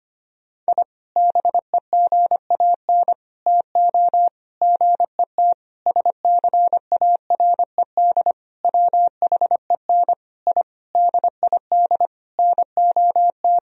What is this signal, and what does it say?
Morse code, heard as a keyed tone, about 25 wpm, 700 Hz. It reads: I BEGAN TO GET SCARED WHEN I DID NOT
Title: Sentences from Top 1000 Words - Review Part 4 - 25wpm